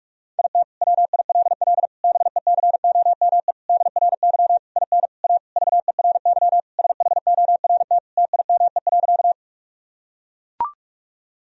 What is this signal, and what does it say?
Morse code, heard as a keyed tone, about 41 wpm, 700 Hz. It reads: IT WILL BECOME DRY IN A VERY SHORT TIME.
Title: Sentences from Top 1000 Words - Review Part 3 - 45wpm